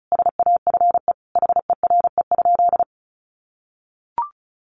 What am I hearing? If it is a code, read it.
Morse code, heard as a keyed tone, about 34 wpm, 700 Hz. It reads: SAFE HERE?
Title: Sentences from Top 1300 Words 35wpm